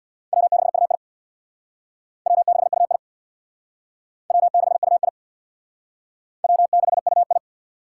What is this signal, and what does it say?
Morse code, heard as a keyed tone, about 60 wpm, 700 Hz. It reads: W6UI W6UI W6UI W6UI
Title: Speed Racing - USA 1x2 Call Signs 50wpm